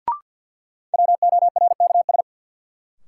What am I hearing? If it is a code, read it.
Morse code, heard as a keyed tone, about 52 wpm, 700 Hz. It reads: WORKS
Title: Sentences from Top 1000 Words - Review Part 5 - New Word Drill - 50wpm